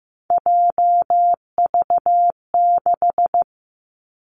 Morse code, heard as a keyed tone, about 15 wpm, 700 Hz. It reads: JV6
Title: Three Challenging Letter-Number Combinations 15wpm